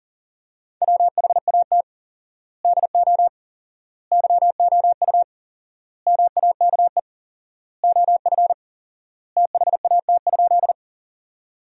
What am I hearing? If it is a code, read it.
Morse code, heard as a keyed tone, about 40 wpm, 700 Hz. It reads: WHAT DO YOU MAKE OF THAT?
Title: 4x Word-Spacing - Sentences from Top 100 Words - 40wpm